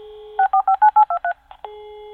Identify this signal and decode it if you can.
DTMF, heard as telephone touch tones, tones of about 80 ms. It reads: 45 23
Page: computer-modem